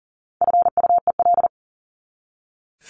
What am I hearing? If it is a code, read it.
Morse code, heard as a keyed tone, about 40 wpm, 700 Hz. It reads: FUEL
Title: Top 1751-2000 Words 40wpm